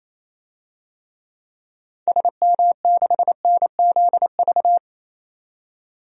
Morse code, heard as a keyed tone, about 28 wpm, 700 Hz. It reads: SM6NZV
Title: DXCC-100 Call Signs - 6 Characters - 28wpm